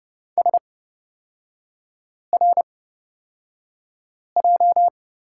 Morse code, heard as a keyed tone, about 30 wpm, spaced as for 5 wpm, 700 Hz. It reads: SRJ
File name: Lesson 38e - Letter q - Three Characters 30wpm